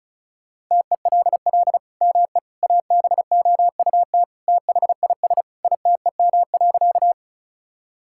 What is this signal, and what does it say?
Morse code, heard as a keyed tone, about 34 wpm, 700 Hz. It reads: TELL ME ABOUT THIS ITEM.
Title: Sentences from Top 1000 Words - Review Part 6 - 35wpm